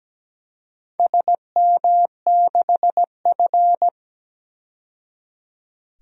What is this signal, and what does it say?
Morse code, heard as a keyed tone, about 17 wpm, 700 Hz. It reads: SM6F
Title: DXCC-100 Call Signs - 4 Characters - 17wpm